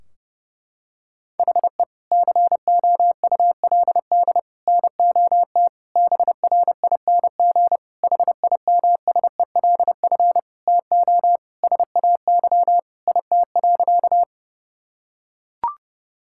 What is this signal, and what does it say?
Morse code, heard as a keyed tone, about 30 wpm, 700 Hz. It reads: HE COULD NOT BRING HIMSELF TO SAY IT.
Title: Sentences from Top 1600 Words 30wpm